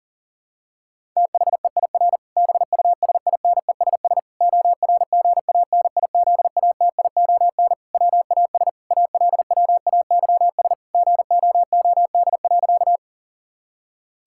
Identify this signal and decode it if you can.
Morse code, heard as a keyed tone, about 40 wpm, 700 Hz. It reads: THEIR BUSINESS ORGANIZATION WAS ALWAYS GOOD.
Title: Sentences from Top 1000 Words - Review Part 5 - 40wpm